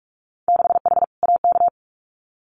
Morse code, heard as a keyed tone, about 46 wpm, 700 Hz. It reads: 6H AK
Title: Winter Field Day Exchanges Encore 45wpm